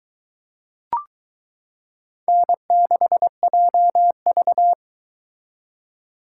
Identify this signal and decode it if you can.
Morse code, heard as a keyed tone, about 22 wpm, 700 Hz. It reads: N6JV